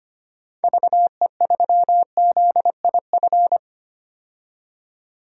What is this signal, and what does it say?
Morse code, heard as a keyed tone, about 25 wpm, 700 Hz. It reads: VE3ZIF
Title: DXCC-100 Call Signs - 6 Characters - 25wpm